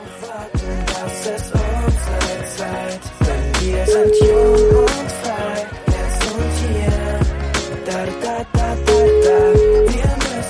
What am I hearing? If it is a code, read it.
Ringback tone (Europe, 425 Hz), with noise.